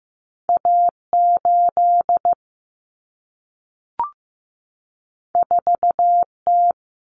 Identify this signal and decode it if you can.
Morse code, heard as a keyed tone, about 15 wpm, 700 Hz. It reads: A8 4T
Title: Common Cut Numbers - 2-Digit 15wpm